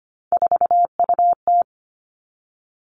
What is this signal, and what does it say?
Morse code, heard as a keyed tone, about 25 wpm, 700 Hz. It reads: 4UT
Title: Three Challenging Letter-Number Combinations 25wpm